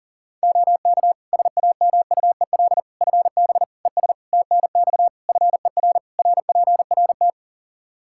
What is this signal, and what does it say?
Morse code, heard as a keyed tone, about 37 wpm, 700 Hz. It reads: OK SAMUEL FB ES TNX FER RPRT
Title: K1USN SST QSO Practice 40wpm